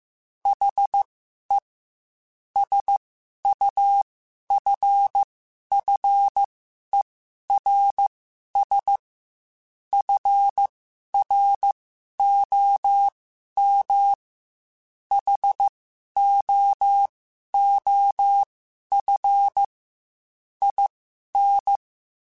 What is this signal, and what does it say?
Morse code, heard as a keyed tone, about 15 wpm, spaced as for 11 wpm, 800 Hz. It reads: HE SUFFERS FROM HOOF IN
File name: Quotes / Quotes-mp3s / HoofInMouth @13